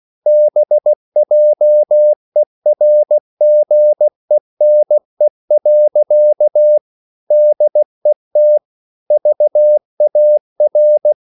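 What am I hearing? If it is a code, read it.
Morse code, heard as a keyed tone, about 16 wpm, 600 Hz. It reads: BJERGENE. DET VAR
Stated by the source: Slow (QRS) Morse code at 16 wpm, from CW med Gnister.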